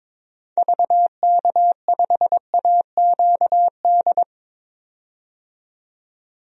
Morse code, heard as a keyed tone, about 22 wpm, 700 Hz. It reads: VK5AQD
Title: DXCC-100 Call Signs - 6 Characters - 22wpm